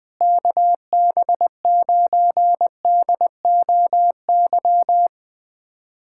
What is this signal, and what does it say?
Morse code, heard as a keyed tone, about 20 wpm, 700 Hz. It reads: KB9DOY
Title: USA Callsign DE Callsign - 6 Characters - 20wpm